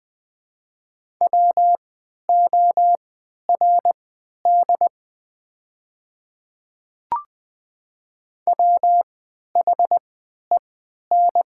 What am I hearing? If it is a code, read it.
Morse code, heard as a keyed tone, about 20 wpm, spaced as for 11 wpm, 700 Hz. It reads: WORD WHEN
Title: 3x Character Spacing - Top 100 Words - 20wpm